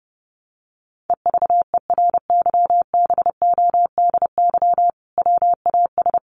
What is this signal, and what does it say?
Morse code, heard as a keyed tone, about 30 wpm, 700 Hz. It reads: EVERYBODY WAS
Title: Sentences from Top 1400 Words 30wpm